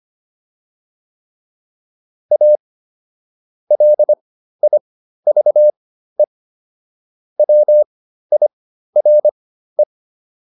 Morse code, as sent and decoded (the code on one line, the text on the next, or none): .- / .-.. .. ...- . / .-- .. .-. .
A LIVE WIRE